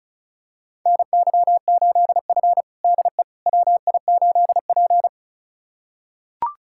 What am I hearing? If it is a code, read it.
Morse code, heard as a keyed tone, about 36 wpm, 700 Hz. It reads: NY8F DE WI8P
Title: USA Callsign DE Callsign - 4 Characters - 35wpm